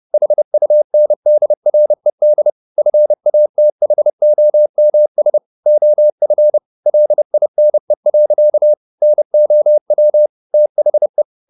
Morse code, heard as a keyed tone, about 30 wpm, 600 Hz. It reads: HUNDRED FATHOMS OF LINE. NOW THE
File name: the old man and the sea / the old man and the sea 1 30wpm